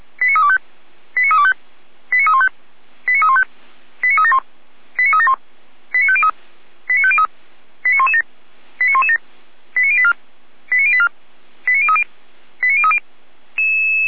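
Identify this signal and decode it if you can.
Five-tone selective call (ZVEI), heard as quick runs of five tones, tones of about 70 ms. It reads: 89325 89325 89315 89315 89371 89371 89503 89503 89117 89117 89084 89084 89030 89030
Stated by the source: ZVEI five-tone selective call received on an SDR, from the Feuerwehr Orlamünde fire brigade audio server.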